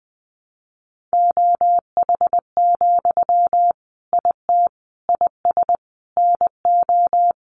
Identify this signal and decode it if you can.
Morse code, heard as a keyed tone, about 20 wpm, 700 Hz. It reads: OH, IT IS NO
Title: Sentences from Top 1200 Words 20wpm